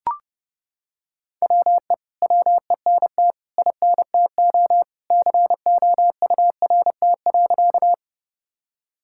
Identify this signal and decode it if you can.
Morse code, heard as a keyed tone, about 32 wpm, 700 Hz. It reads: WE WENT INTO COURT.
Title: Sentences from Top 1600 Words 30wpm